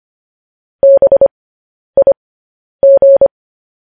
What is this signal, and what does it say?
Morse code, heard as a keyed tone, about 25 wpm, spaced as for 10 wpm, 550 Hz. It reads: BIG
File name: podcast / News headlines-1x-25x10